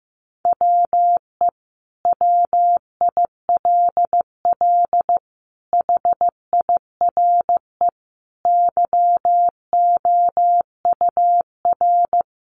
Morse code, heard as a keyed tone, about 15 wpm, 700 Hz. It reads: WE WILL HIRE YOUR
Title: Sentences from Top 1400 Words 15wpm